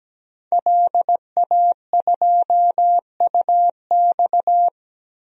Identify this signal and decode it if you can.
Morse code, heard as a keyed tone, about 17 wpm, 700 Hz. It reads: LA2UX
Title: DXCC-100 Call Signs - 5 Characters - 17wpm